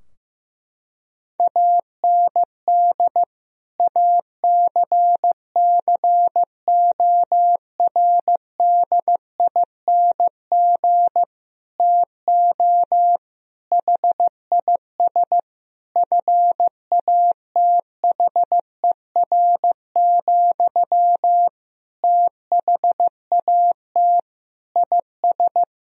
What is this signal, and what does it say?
Morse code, heard as a keyed tone, about 16 wpm, 700 Hz. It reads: AND ACCORDING TO HIS FATHER, THAT IS
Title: Sentences from Top 1000 Words - Review Part 4 - 15wpm